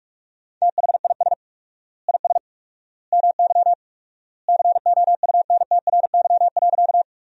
Morse code, heard as a keyed tone, about 41 wpm, 700 Hz. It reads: THIS IS MY COUNTRY.
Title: 4x Word-Spacing - Sentences from Top 300 Words - 45wpm